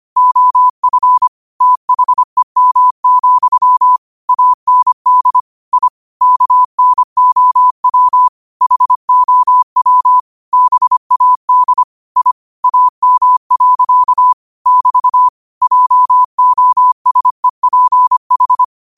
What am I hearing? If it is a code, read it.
Morse code, heard as a keyed tone, about 25 wpm, 1000 Hz. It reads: OF THEM, AND I KNOW HOW BAD I AM. <BT> JOSEPH
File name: qotdfiles / 25wpm